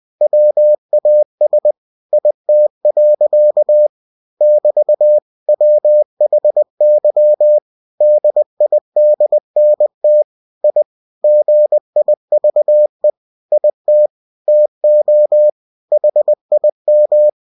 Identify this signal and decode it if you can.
Morse code, heard as a keyed tone, about 20 wpm, 600 Hz. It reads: WAS IT. <BT> WHY DIDNT I GIVE IT TO HIM